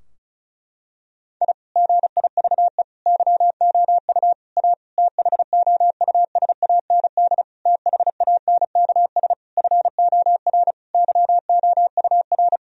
Morse code, heard as a keyed tone, about 34 wpm, 700 Hz. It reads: I GIVE YOU A THOUSAND THANKS FOR YOUR
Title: Sentences from Top 1000 Words - Review Part 4 - 35wpm